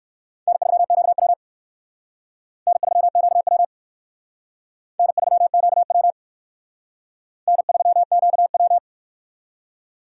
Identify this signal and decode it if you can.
Morse code, heard as a keyed tone, about 56 wpm, 700 Hz. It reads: N3QW N3QW N3QW N3QW
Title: Speed Racing - USA 1x2 Call Signs 45wpm